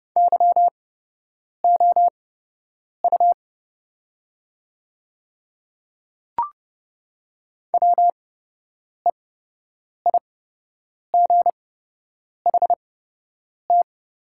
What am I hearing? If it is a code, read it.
Morse code, heard as a keyed tone, about 30 wpm, spaced as for 8 wpm, 700 Hz. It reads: YOU WEIGHT
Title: 8x Character Spacing - Top 500 Words - 30wpm